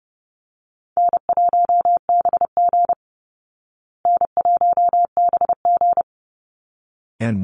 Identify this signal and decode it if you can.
Morse code, heard as a keyed tone, about 30 wpm, 700 Hz. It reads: N1BG N1BG
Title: Easy Call Signs Repeated 2x - 30wpm